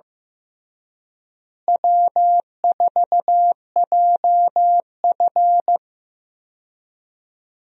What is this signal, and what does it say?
Morse code, heard as a keyed tone, about 16 wpm, 700 Hz. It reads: W4JF
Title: USA Call Signs - 4 Characters - Repeated 3x - 15wpm